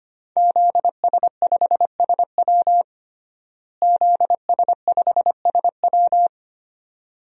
Morse code, heard as a keyed tone, about 25 wpm, 700 Hz. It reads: ZS5SW ZS5SW
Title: DXCC-100 Call Signs - 5 Characters - Repeated 2x - 25wpm